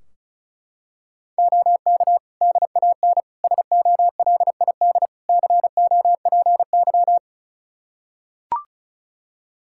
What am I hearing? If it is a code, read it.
Morse code, heard as a keyed tone, about 36 wpm, 700 Hz. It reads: OK DAN SOLID COPY
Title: Practice QSO - Exchange 3 - More Info - 35wpm